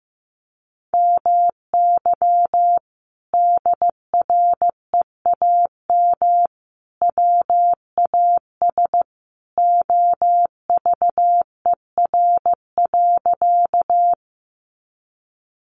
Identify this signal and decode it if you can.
Morse code, heard as a keyed tone, about 15 wpm, 700 Hz. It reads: MY DREAM WAS OVER.
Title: Sentences from Top 1500 Words 15wpm